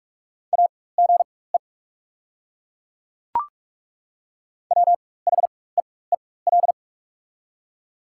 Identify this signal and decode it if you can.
Morse code, heard as a keyed tone, about 42 wpm, spaced as for 21 wpm, 690 Hz. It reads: AGE WHEEL
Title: 4x Character Spacing - Top 500 Words - 45wpm